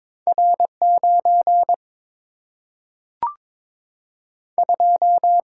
Morse code, heard as a keyed tone, about 22 wpm, 700 Hz. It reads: R 9 2
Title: Single-Letter Single-Number 22wpm